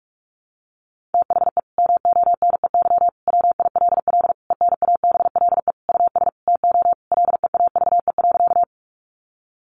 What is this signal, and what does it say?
Morse code, heard as a keyed tone, about 44 wpm, 700 Hz. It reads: THE MONEY WILL ENABLE US TO LEAVE.
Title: Sentences from Top 1500 Words 45wpm